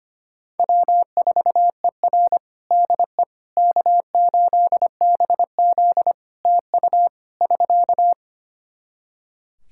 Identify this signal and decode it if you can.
Morse code, heard as a keyed tone, about 25 wpm, 700 Hz. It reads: W4ER DE K8BZ TU <SK>
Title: Practice QSO - Exchange 4 - Ending - 25wpm